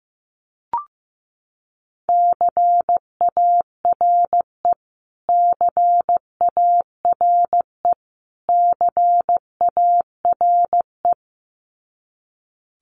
Morse code, heard as a keyed tone, about 15 wpm, 700 Hz. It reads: CARE CARE CARE
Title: Top 500 Words Repeated 15wpm